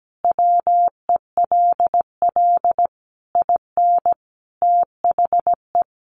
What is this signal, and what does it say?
Morse code, heard as a keyed tone, about 17 wpm, 700 Hz. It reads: WELL IN THE
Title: Sentences from Top 1200 Words 17wpm